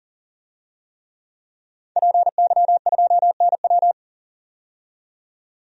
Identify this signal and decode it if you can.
Morse code, heard as a keyed tone, about 40 wpm, 700 Hz. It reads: PY2NW
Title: DXCC-100 Call Signs - 5 Characters - 40wpm